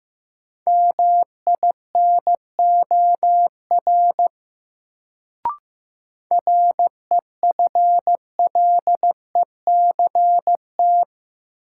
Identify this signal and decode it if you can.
Morse code, heard as a keyed tone, about 15 wpm, 700 Hz. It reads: MINOR REFLECT
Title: Sentences from Top 1500 Words - New Word Drill Alt - 15wpm